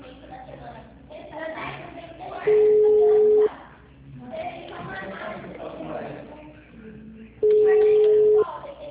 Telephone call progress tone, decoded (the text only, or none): ringback tone